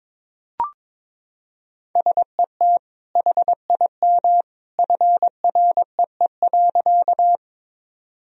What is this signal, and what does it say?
Morse code, heard as a keyed tone, about 22 wpm, 700 Hz. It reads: SET HIM FREE.
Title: Sentences from Top 1000 Words - Review Part 1 - 22wpm